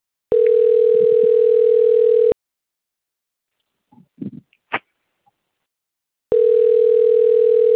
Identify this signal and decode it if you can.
Ringback tone (North America), 440 + 480 Hz.